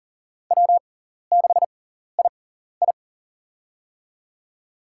Morse code, heard as a keyed tone, about 40 wpm, spaced as for 14 wpm, 700 Hz. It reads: W6II
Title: USA Call Signs - 6x Character Spacing - 40wpm